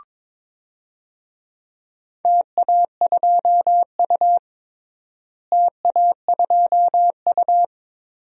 Morse code, heard as a keyed tone, about 22 wpm, 700 Hz. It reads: TA2U TA2U
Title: DXCC-100 Call Signs - 4 Characters - Repeated 3x - 22wpm